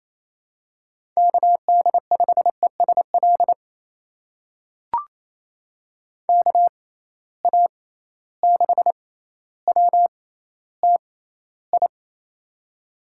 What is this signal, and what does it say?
Morse code, heard as a keyed tone, about 28 wpm, spaced as for 14 wpm, 700 Hz. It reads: KD5ESL KA6WTI
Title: USA Call Signs - 6 Characters - 6x Character Spacing - 28wpm